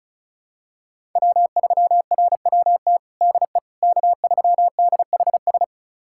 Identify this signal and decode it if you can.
Morse code, heard as a keyed tone, about 36 wpm, 700 Hz. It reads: W3RWT DE K3DHS
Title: USA Callsign DE Callsign - 5 Characters - 35wpm